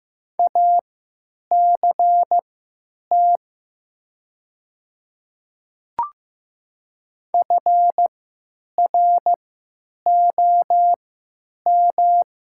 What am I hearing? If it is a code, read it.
Morse code, heard as a keyed tone, about 15 wpm, spaced as for 9 wpm, 700 Hz. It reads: ACT FROM
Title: 3x Character Spacing - Top 200 Words - 15wpm